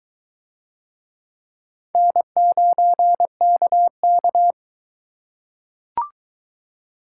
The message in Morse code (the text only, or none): N9KK